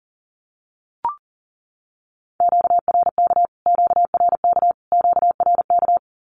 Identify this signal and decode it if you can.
Morse code, heard as a keyed tone, about 40 wpm, 700 Hz. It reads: QRK QRK QRK
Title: Q-Codes Repeated 3x 40wpm